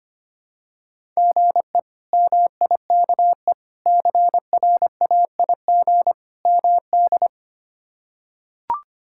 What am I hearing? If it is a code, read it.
Morse code, heard as a keyed tone, about 25 wpm, 700 Hz. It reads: GE MIKE CRAIG MD
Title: K1USN SST QSO Practice - 25wpm